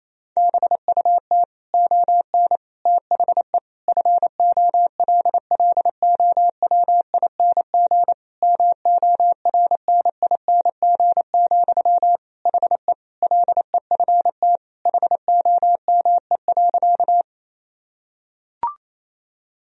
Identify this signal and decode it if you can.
Morse code, heard as a keyed tone, about 28 wpm, 700 Hz. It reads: BUT ON THE FOLLOWING MORNING, HE LEFT HOME.
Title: Sentences from Top 1000 Words - Review Part 4 - 28wpm